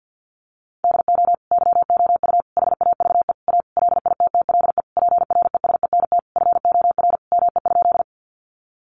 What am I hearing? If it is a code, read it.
Morse code, heard as a keyed tone, about 50 wpm, 700 Hz. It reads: DO YOU HAVE A LITTLE PRESENT FOR ME?
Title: Sentences from Top 800 Words 50wpm